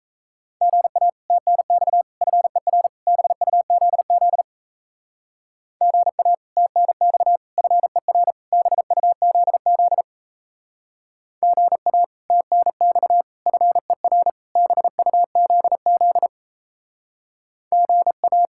Morse code, heard as a keyed tone, about 37 wpm, 700 Hz. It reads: GA TNX FER BUZZ GA TNX FER BUZZ GA TNX FER BUZZ GA